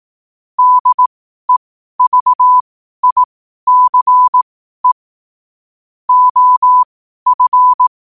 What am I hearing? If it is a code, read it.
Morse code, heard as a keyed tone, about 18 wpm, spaced as for 12 wpm, 1000 Hz. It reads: DEVICE OF
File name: qotdfiles / 12wpm